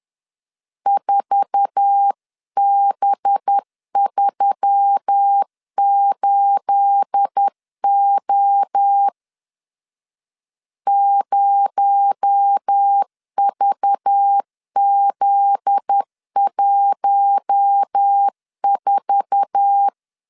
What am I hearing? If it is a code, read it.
Morse code, heard as a keyed tone, about 10 wpm, 800 Hz. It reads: 4B38O 0VZ14
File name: CW / DET / 1 9 1500C S B 10wpm 4439